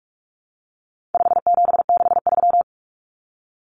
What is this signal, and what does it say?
Morse code, heard as a keyed tone, about 44 wpm, 700 Hz. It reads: 5763
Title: Numbers - 4-Digit 45wpm